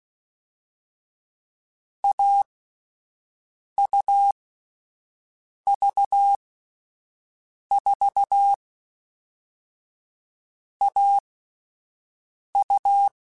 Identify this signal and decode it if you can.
Morse code, heard as a keyed tone, about 16 wpm, 800 Hz. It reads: A U V 4 A U
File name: Difficult Combos / Difficult Combos-mp3s / AUV AUV4 @10